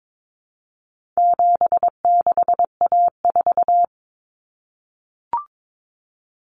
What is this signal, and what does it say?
Morse code, heard as a keyed tone, about 22 wpm, 700 Hz. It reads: 76A4
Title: Common Cut Numbers - 4-Digit 22wpm